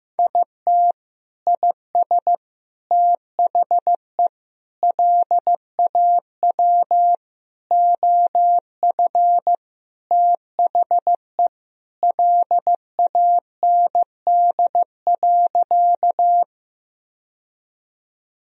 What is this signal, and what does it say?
Morse code, heard as a keyed tone, about 15 wpm, 700 Hz. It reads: IT IS THE LAW OF THE LAND.
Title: Sentences from Top 1000 Words - Review Part 3 - 15wpm